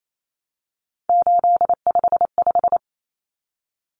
Morse code, heard as a keyed tone, about 28 wpm, 700 Hz. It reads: 855
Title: Common Cut Numbers - 3-Digit 28wpm